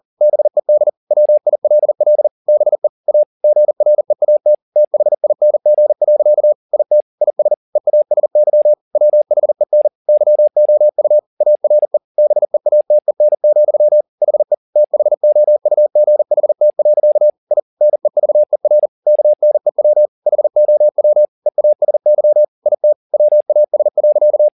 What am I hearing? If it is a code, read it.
Morse code, heard as a keyed tone, about 40 wpm, 600 Hz. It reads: BED WILL BE A GREAT THING. IT IS EASY WHEN YOU ARE BEATEN, HE THOUGHT. I NEVER KNEW HOW EASY IT WAS.
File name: the old man and the sea 2 40wpm